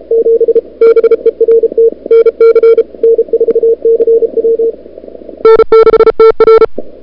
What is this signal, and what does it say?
Morse code, heard as a keyed tone, about 37 wpm, 440 Hz. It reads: 7 BERT NC N4CW N6TR
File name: CWSprint / February 2007 CW Sprint / 01-2007 CW Sprint Hour1